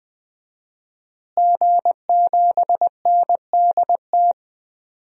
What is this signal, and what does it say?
Morse code, heard as a keyed tone, about 20 wpm, 700 Hz. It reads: G7NDT
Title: DXCC-100 Call Signs - 5 Characters - Repeated 2x - 20wpm